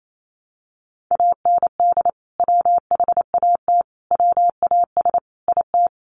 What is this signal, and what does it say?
Morse code, heard as a keyed tone, about 28 wpm, 700 Hz. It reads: AND WHAT WAS IT
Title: Sentences from Top 100 Words 28wpm